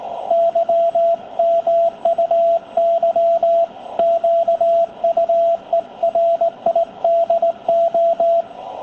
Morse code, heard as a keyed tone, about 17 wpm, 720 Hz, with radio noise. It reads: Y MUY QUERIDO
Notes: Encuentro de radioaficionados en 7030, banda de 40 m, batiendo manipuladores, para despedir a Tony, CA2LQA, de Villa Alemana, Valparaíso, Chile.